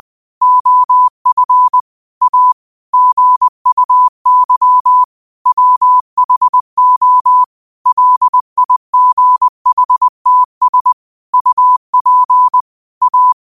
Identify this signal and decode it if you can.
Morse code, heard as a keyed tone, about 20 wpm, 1000 Hz. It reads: OF A GUY WHO LIGHTS UP A